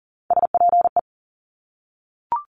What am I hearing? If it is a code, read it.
Morse code, heard as a keyed tone, about 38 wpm, 700 Hz. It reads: SPE
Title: Top Three Letter Combinations in Words 40wpm